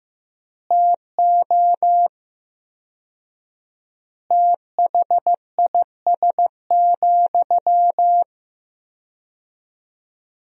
Morse code, heard as a keyed tone, about 15 wpm, 700 Hz. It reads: TO THIS,
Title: 4x Word-Spacing - Sentences from Top 400 Words - 15wpm